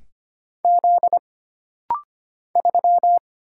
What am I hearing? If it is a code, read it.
Morse code, heard as a keyed tone, about 25 wpm, 700 Hz. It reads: Z 3
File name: Single Character (Letter:Number:Punctuation:Prosign) - Rapid-Fire - 25wpm